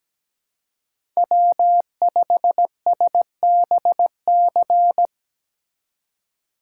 Morse code, heard as a keyed tone, about 20 wpm, 700 Hz. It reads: W5SBC
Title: Speed Racing - USA 1x3 Call Signs 17wpm